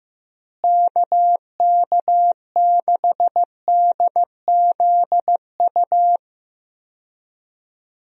Morse code, heard as a keyed tone, about 15 wpm, 700 Hz. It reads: KK6DZU
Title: USA Call Signs - 6 Characters - Repeated 3x - 15wpm